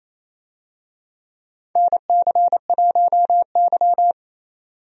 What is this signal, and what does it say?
Morse code, heard as a keyed tone, about 27 wpm, 700 Hz. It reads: NC1Y
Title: USA Call Signs - 4 Characters - 28wpm